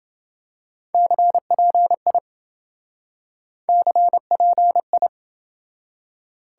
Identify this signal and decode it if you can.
Morse code, heard as a keyed tone, about 27 wpm, 700 Hz. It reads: CPI CPI